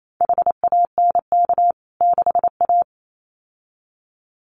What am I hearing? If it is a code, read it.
Morse code, heard as a keyed tone, about 28 wpm, 700 Hz. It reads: HANK 6A
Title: CWT Exchange 28wpm